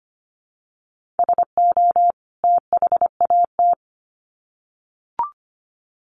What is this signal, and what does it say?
Morse code, heard as a keyed tone, about 25 wpm, 700 Hz. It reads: SO THAT
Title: Sets of 2 Words 25wpm